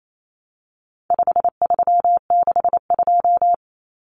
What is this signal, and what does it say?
Morse code, heard as a keyed tone, about 28 wpm, 700 Hz. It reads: 5362
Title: Common Cut Numbers - 4-Digit 28wpm